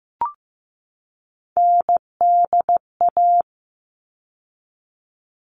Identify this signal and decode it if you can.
Morse code, heard as a keyed tone, about 15 wpm, 700 Hz. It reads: NDA
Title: All Cut Numbers - 3-Digit 15wpm